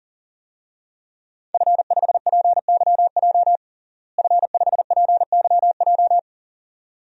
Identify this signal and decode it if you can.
Morse code, heard as a keyed tone, about 40 wpm, 700 Hz. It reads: F5PYJ F5PYJ
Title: DXCC-100 Call Signs - 5 Characters - Repeated 2x - 40wpm